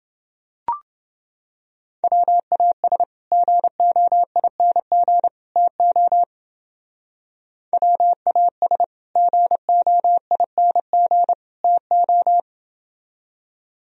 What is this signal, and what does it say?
Morse code, heard as a keyed tone, about 28 wpm, 700 Hz. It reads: WAS GOING TO WAS GOING TO